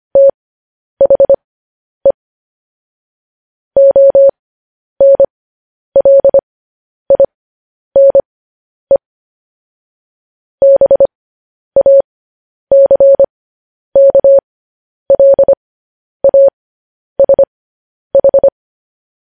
Morse code, heard as a keyed tone, about 25 wpm, spaced as for 10 wpm, 550 Hz. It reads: THE ONLINE BACKLASH
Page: news-headlines-in-morse-code-at-10-wpm